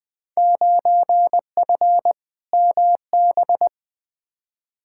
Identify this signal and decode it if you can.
Morse code, heard as a keyed tone, about 20 wpm, 700 Hz. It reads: 9F MB
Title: Field Day Exchanges Encore - 20wpm